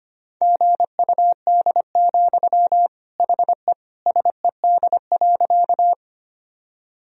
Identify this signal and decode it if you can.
Morse code, heard as a keyed tone, about 25 wpm, 700 Hz. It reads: GUD, HE SED.
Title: Sentences from Top 200 Words and Abbreviations 25wpm